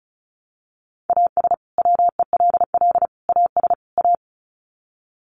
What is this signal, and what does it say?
Morse code, heard as a keyed tone, about 34 wpm, 700 Hz. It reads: AS WELL AS A
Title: Sets of 4 Words 35wpm